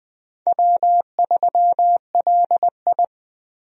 Morse code, heard as a keyed tone, about 21 wpm, 700 Hz. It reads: W3LI
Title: USA Call Signs - 4 Characters - 20wpm